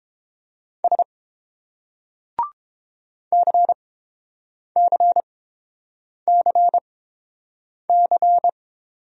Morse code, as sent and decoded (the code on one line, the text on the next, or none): ... / -.-. -.-. -.-. -.-.
S CCCC